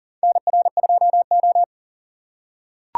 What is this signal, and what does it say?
Morse code, heard as a keyed tone, about 40 wpm, 700 Hz. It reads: NR2O
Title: USA Call Signs - 4 Characters - 4x Character Spacing - 40wpm